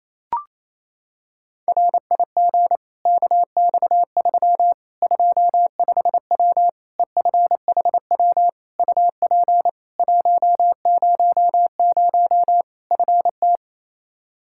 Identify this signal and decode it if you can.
Morse code, heard as a keyed tone, about 28 wpm, 700 Hz. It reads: RIG KX3 25W EFHW UP 100 FT
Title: Practice QSO - Exchange 2 - Info - 28wpm